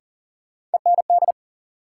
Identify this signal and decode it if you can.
Morse code, heard as a keyed tone, about 40 wpm, 700 Hz. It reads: END
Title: Sentences from Top 200 Words - New Word Drill - 40wpm